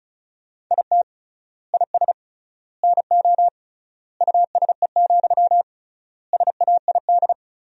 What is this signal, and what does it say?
Morse code, heard as a keyed tone, about 34 wpm, 700 Hz. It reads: IT IS NO USE, SAID
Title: 3x Word-Spacing - Sentences from Top 500 Words - 35wpm